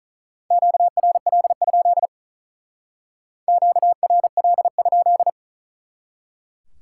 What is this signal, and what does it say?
Morse code, heard as a keyed tone, about 46 wpm, 700 Hz. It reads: QRL? QRL?